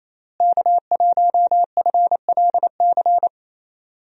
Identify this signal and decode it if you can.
Morse code, heard as a keyed tone, about 28 wpm, 700 Hz. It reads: K1FLC
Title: USA Call Signs - 5 Characters - Repeated 2x - 28wpm